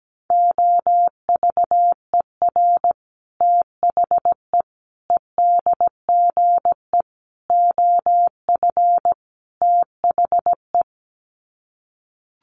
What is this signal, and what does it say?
Morse code, heard as a keyed tone, about 17 wpm, 700 Hz. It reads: OVER THE EDGE OF THE
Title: Sets of 5 Words Encore2 17wpm